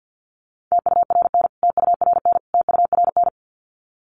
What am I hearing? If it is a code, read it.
Morse code, heard as a keyed tone, about 50 wpm, 700 Hz. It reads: TURN TURN TURN
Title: Top 300 Words Repeated 50wpm